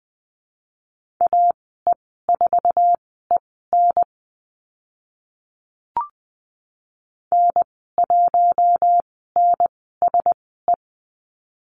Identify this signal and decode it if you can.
Morse code, heard as a keyed tone, about 20 wpm, spaced as for 14 wpm, 700 Hz. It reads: AE4EN N1NSE
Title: 062 Lesson 11g - Letter h - Call Signs 20wpm